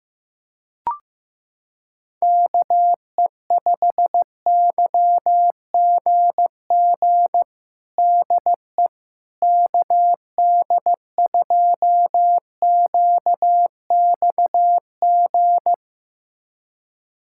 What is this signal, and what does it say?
Morse code, heard as a keyed tone, about 16 wpm, 700 Hz. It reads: KE5YGG DE KD2QXG
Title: USA Callsign DE Callsign - 6 Characters - 15wpm